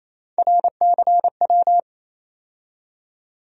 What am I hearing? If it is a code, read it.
Morse code, heard as a keyed tone, about 28 wpm, 700 Hz. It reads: RCW
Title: USA Call Sign Postfixes - 3 Letters 28wpm